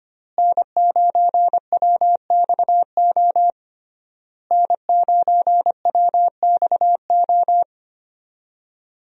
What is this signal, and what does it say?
Morse code, heard as a keyed tone, about 25 wpm, 700 Hz. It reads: N9WXO N9WXO
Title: USA Call Signs - 5 Characters - Repeated 3x - 25wpm